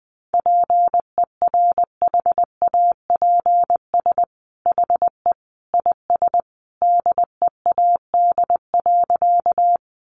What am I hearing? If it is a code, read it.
Morse code, heard as a keyed tone, about 20 wpm, 700 Hz. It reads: PERHAPS HE IS DEAD.
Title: Sentences from Top 1500 Words 20wpm